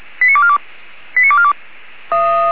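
Five-tone selective call (ZVEI), heard as quick runs of five tones, tones of about 70 ms. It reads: 89242 89242
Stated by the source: Five-tone ZVEI selective call received by SDR.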